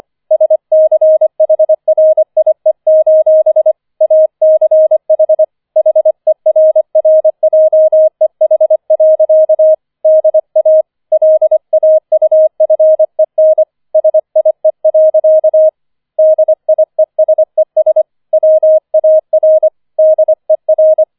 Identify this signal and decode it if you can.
Morse code, heard as a keyed tone, about 25 wpm, 630 Hz. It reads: SCHRIE: ACH HERRJEH. DA LAUFEN SIE. DIESES WAR DER
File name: 25wpm